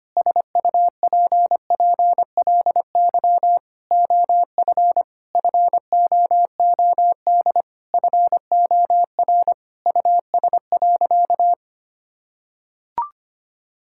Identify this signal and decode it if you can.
Morse code, heard as a keyed tone, about 25 wpm, 700 Hz. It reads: SUPPLY OF FOOD FOR US.
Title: Sentences from Top 1000 Words - Review Part 5 - 25wpm